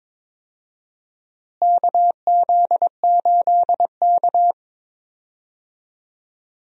Morse code, heard as a keyed tone, about 23 wpm, 700 Hz. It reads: KZ8K
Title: USA Call Signs - 4 Characters - Repeated 3x - 22wpm